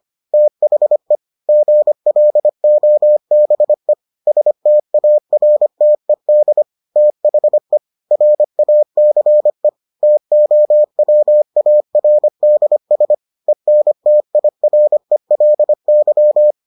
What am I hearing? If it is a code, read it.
Morse code, heard as a keyed tone, about 25 wpm, 600 Hz. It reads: THE GLOBE STARTED THE RACE TOWARDS ENTIRELY